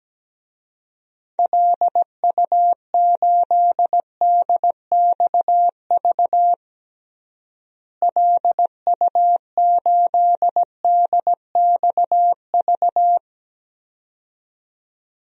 Morse code, heard as a keyed tone, about 17 wpm, 700 Hz. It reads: LU8DXV LU8DXV
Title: DXCC-100 Call Signs - 6 Characters - Repeated 2x - 17wpm